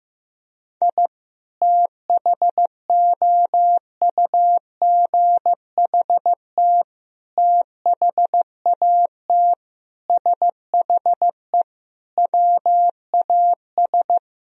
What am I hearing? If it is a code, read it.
Morse code, heard as a keyed tone, about 15 wpm, 700 Hz. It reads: I THOUGHT THAT SHE WAS
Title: Sentences from Top 1000 Words - Review Part 3 - 15wpm